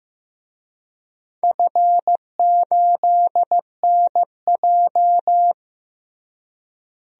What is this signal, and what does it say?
Morse code, heard as a keyed tone, about 15 wpm, 700 Hz. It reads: F8NJ
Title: DXCC-100 Call Signs - 4 Characters - Repeated 3x - 15wpm